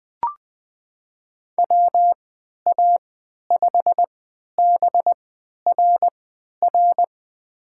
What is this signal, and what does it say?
Morse code, heard as a keyed tone, about 20 wpm, spaced as for 11 wpm, 700 Hz. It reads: WA5BRR
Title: USA Call Signs - 6 Characters - 3x Character Spacing - 20wpm